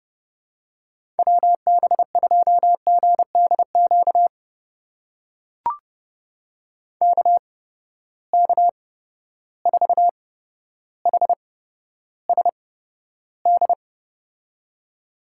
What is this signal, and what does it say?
Morse code, heard as a keyed tone, about 30 wpm, spaced as for 13 wpm, 700 Hz. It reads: WB2GDQ KK4HSD
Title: USA Call Signs - 8x Character Spacing - 30wpm